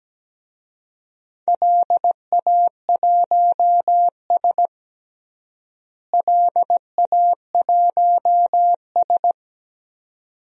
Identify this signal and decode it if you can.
Morse code, heard as a keyed tone, about 17 wpm, 700 Hz. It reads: LA1S LA1S
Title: DXCC-100 Call Signs - 4 Characters - Repeated 2x - 17wpm